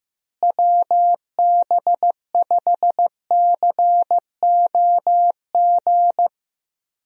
Morse code, heard as a keyed tone, about 16 wpm, 700 Hz. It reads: WB5COG
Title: USA Call Signs - 6 Characters - Repeated 3x - 15wpm